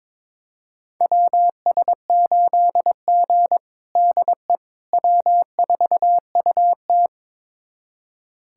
Morse code, heard as a keyed tone, about 21 wpm, 700 Hz. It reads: WS8G DE W4UT
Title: USA Callsign DE Callsign - 4 Characters - 22wpm